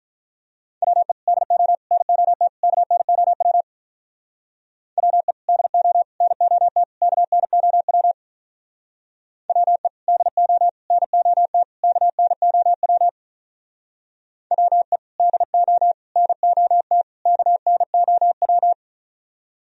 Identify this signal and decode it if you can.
Morse code, heard as a keyed tone, about 44 wpm, 700 Hz. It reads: WE DO NOT KNOW WE DO NOT KNOW WE DO NOT KNOW WE DO NOT KNOW